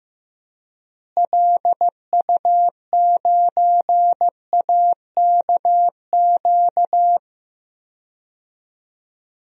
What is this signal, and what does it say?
Morse code, heard as a keyed tone, about 15 wpm, 700 Hz. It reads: LU9AKQ
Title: DXCC-100 Call Signs - 6 Characters - Repeated 3x - 15wpm